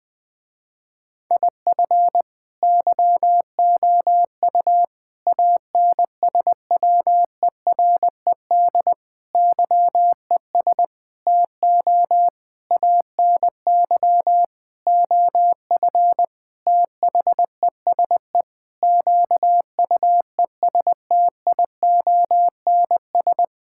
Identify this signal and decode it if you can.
Morse code, heard as a keyed tone, about 20 wpm, 700 Hz. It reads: IF YOU ANSWERED YES TO ANY OF THESE QUESTIONS
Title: Sentences with 9-Word Phrases - Spoken Beginning - 20wpm